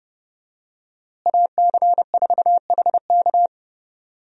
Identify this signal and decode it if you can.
Morse code, heard as a keyed tone, about 32 wpm, 700 Hz. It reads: AC4HK
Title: USA Call Signs - 5 Characters - Repeated 2x - 30wpm